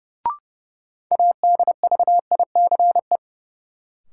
Morse code, heard as a keyed tone, about 32 wpm, 700 Hz. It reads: ADVICE
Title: Sentences from Top 1100 Words - New Word Drill - 30wpm